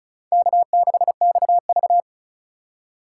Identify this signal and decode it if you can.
Morse code, heard as a keyed tone, about 36 wpm, 700 Hz. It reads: K6XV
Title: USA Call Signs - 4 Characters - 3x Character Spacing - 35wpm